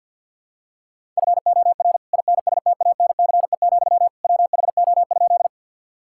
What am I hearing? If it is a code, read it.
Morse code, heard as a keyed tone, about 49 wpm, 690 Hz. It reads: FOR INSTANCE, WHO?
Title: Sentences from Top 1000 Words - Review Part 5 - 50wpm